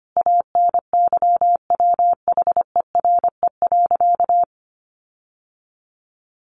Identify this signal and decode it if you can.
Morse code, heard as a keyed tone, about 25 wpm, 700 Hz. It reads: ANYWHERE.
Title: Sentences from Top 1400 Words 25wpm